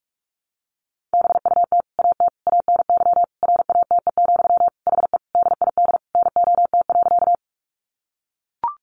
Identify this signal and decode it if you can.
Morse code, heard as a keyed tone, about 44 wpm, 700 Hz. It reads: BUT AT ANY RATE, HE DID NOT.
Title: Sentences from Top 700 Words 45wpm